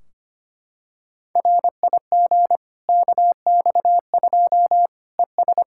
Morse code, heard as a keyed tone, about 25 wpm, 700 Hz. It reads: RIG KX2 ES
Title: K1USN SST QSO Practice 25wpm